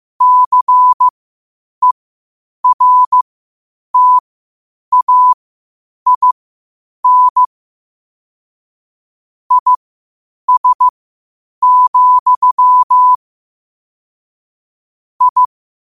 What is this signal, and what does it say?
Morse code, heard as a keyed tone, about 15 wpm, spaced as for 8 wpm, 1000 Hz. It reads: CERTAIN IS, I
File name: qotdfiles / 8wpm